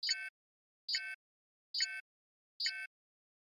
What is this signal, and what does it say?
There is no code.